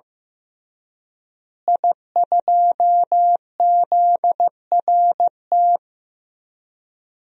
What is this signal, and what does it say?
Morse code, heard as a keyed tone, about 15 wpm, 700 Hz. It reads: I2ZRT
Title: DXCC-100 Call Signs - 5 Characters - Repeated 3x - 15wpm